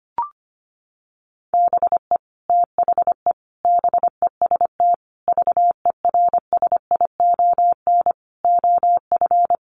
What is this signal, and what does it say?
Morse code, heard as a keyed tone, about 25 wpm, 700 Hz. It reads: BE THE BEST VERSION OF
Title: Sentences from Top 1400 Words 25wpm